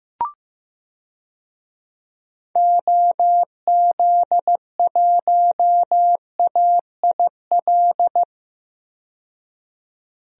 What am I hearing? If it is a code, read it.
Morse code, heard as a keyed tone, about 15 wpm, 700 Hz. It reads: OZ1AIL
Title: DXCC-100 Call Signs - 6 Characters - 15wpm